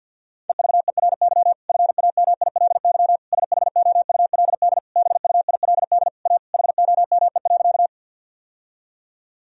Morse code, heard as a keyed tone, about 53 wpm, 700 Hz. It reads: EVERY FAMILY SHOULD BUILD A HOME.
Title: Sentences from Top 1000 Words - Review Part 1 - 50wpm